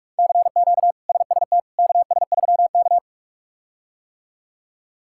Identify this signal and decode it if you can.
Morse code, heard as a keyed tone, about 46 wpm, 700 Hz. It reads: CQ SST KS3K
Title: K1USN SST QSO Practice - 45wpm